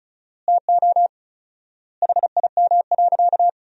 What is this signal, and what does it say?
Morse code, heard as a keyed tone, about 36 wpm, 700 Hz. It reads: TO HIM.
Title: 4x Word-Spacing - Sentences from Top 300 Words - 35wpm